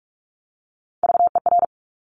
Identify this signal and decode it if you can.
Morse code, heard as a keyed tone, about 46 wpm, 690 Hz. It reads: VER
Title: Top Three Letter Combinations in Words 45wpm